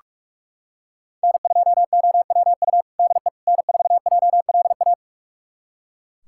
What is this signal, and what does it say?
Morse code, heard as a keyed tone, about 45 wpm, 700 Hz. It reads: N2OWU DE N4JLA
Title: USA Callsign DE Callsign - 5 Characters - 45wpm